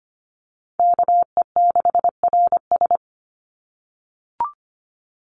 Morse code, heard as a keyed tone, about 25 wpm, 700 Hz. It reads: KE6RS
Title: NA SOTA Chaser Callsigns 25wpm